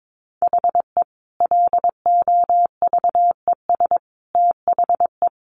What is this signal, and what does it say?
Morse code, heard as a keyed tone, about 22 wpm, 700 Hz. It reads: HE LOVES THE
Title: Additional Easy Phrases 22wpm